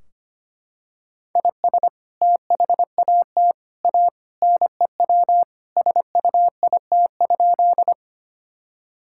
Morse code, heard as a keyed tone, about 25 wpm, 700 Hz. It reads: IS THAT A NEW SUIT?
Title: Sentences from Top 1600 Words 25wpm